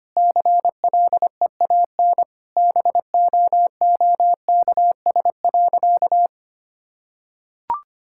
Morse code, heard as a keyed tone, about 25 wpm, 700 Hz. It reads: CLEAN BOOKS.
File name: practice / 25wpm / Sentences with 4-Word Phrases - 25wpm